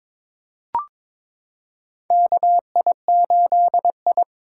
Morse code, heard as a keyed tone, about 22 wpm, 700 Hz. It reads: KI8I
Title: K1USN SST QSO Practice - 22wpm